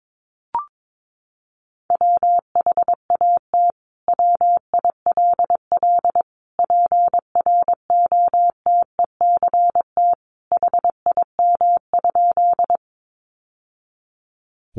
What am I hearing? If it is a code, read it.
Morse code, heard as a keyed tone, about 22 wpm, 700 Hz. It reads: WHAT WILL PROTECT HIM?
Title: Sentences from Top 900 Words 22wpm